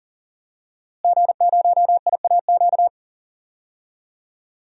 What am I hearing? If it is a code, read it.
Morse code, heard as a keyed tone, about 41 wpm, 700 Hz. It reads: G0IAQ
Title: DXCC-100 Call Signs - 5 Characters - Repeated 3x - 40wpm